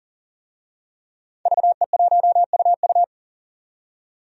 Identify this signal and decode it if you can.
Morse code, heard as a keyed tone, about 38 wpm, 700 Hz. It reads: VE1UU
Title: DXCC-100 Call Signs - 5 Characters - 40wpm